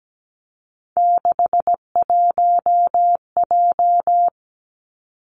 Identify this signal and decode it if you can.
Morse code, heard as a keyed tone, about 17 wpm, 700 Hz. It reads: 61J
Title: Three Challenging Letter-Number Combinations 17wpm